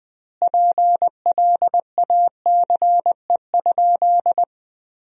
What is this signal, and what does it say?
Morse code, heard as a keyed tone, about 21 wpm, 700 Hz. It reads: PLACE?
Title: 2x Word-Spacing - Sentences from Top 200 Words - 20wpm